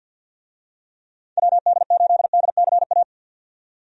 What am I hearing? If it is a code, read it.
Morse code, heard as a keyed tone, about 46 wpm, 700 Hz. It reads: WD8DCA
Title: USA Call Signs - 6 Characters - 50wpm